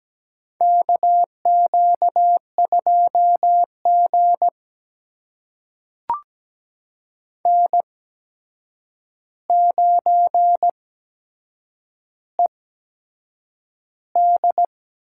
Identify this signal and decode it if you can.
Morse code, heard as a keyed tone, about 17 wpm, spaced as for 7 wpm, 700 Hz. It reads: KQ2G N9ED
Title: USA Call Signs - 4 Characters - 8x Character Spacing - 17wpm